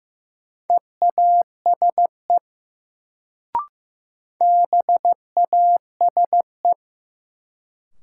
Morse code, heard as a keyed tone, about 15 wpm, 700 Hz. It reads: EASE BASE
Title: Sentences from Top 1000 Words - Review Part 1 - New Word Drill - 15wpm